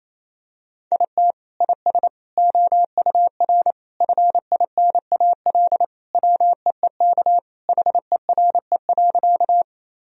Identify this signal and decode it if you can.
Morse code, heard as a keyed tone, about 28 wpm, 700 Hz. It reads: IT IS OUR FINAL WEEK HERE.
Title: Sentences from Top 1000 Words - Review Part 2 - 28wpm